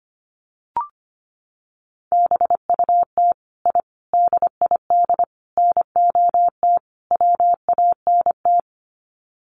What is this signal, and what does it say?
Morse code, heard as a keyed tone, about 25 wpm, 700 Hz. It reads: BUT I DID NOT WANT
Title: Sets of 5 Words Encore 25wpm